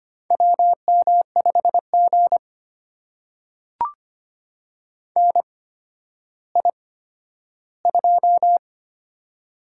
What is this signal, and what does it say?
Morse code, heard as a keyed tone, about 25 wpm, spaced as for 12 wpm, 700 Hz. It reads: WM5G NI2
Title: USA Call Signs - 4 Characters - 8x Character Spacing - 25wpm